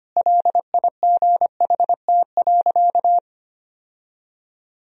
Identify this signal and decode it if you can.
Morse code, heard as a keyed tone, about 25 wpm, 700 Hz. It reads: LIGHT.
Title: Sentences from Top 1000 Words - Review Part 1 - 25wpm